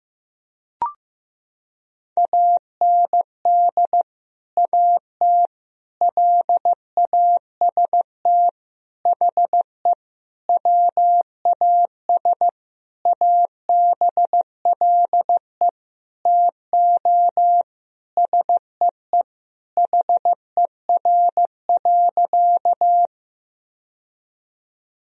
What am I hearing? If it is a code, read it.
Morse code, heard as a keyed tone, about 15 wpm, 700 Hz. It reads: AND AT LAST HE WAS ABLE TO SEE HER.
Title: Sentences from Top 1000 Words - Review Part 2 - 15wpm